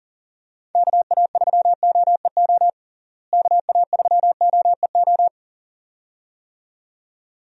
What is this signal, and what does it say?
Morse code, heard as a keyed tone, about 40 wpm, 700 Hz. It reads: KA3OEO KA3OEO
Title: USA Call Signs - 6 Characters - Repeated 2x - 40wpm